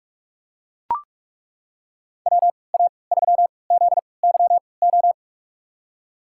Morse code, heard as a keyed tone, about 43 wpm, spaced as for 25 wpm, 700 Hz. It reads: WA3ZYO
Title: USA Call Signs - 6 Characters - 3x Character Spacing - 45wpm